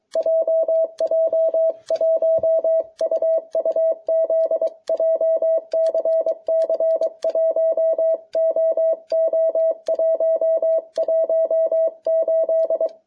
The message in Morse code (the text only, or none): JJ1UUZ JCC 1OO118